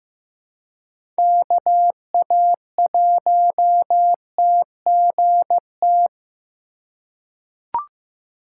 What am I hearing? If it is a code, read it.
Morse code, heard as a keyed tone, about 15 wpm, 700 Hz. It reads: KA1TGT
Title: USA Call Signs - 6 Characters - 3x Character Spacing - 15wpm